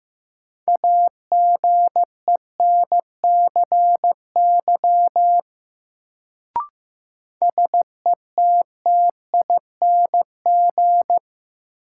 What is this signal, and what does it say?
Morse code, heard as a keyed tone, about 15 wpm, 700 Hz. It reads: AGENCY SETTING
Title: Sentences from Top 1300 Words - New Word Drill Alt - 15wpm